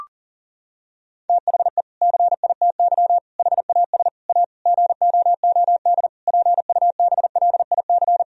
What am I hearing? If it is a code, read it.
Morse code, heard as a keyed tone, about 40 wpm, 700 Hz. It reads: THE CITY HAS A GOOD PUBLIC
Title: Sentences from Top 1000 Words - Review Part 4 - 40wpm